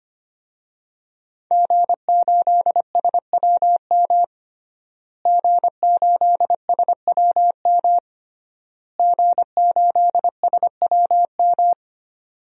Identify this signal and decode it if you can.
Morse code, heard as a keyed tone, about 25 wpm, 700 Hz. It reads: G8SWM G8SWM G8SWM
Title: DXCC-100 Call Signs - 5 Characters - Repeated 3x - 25wpm